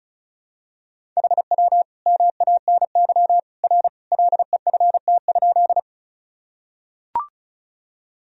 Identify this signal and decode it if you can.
Morse code, heard as a keyed tone, about 36 wpm, 700 Hz. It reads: HW MANY R LEFT?
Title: Sentences from Top 300 Words and Abbreviations 35wpm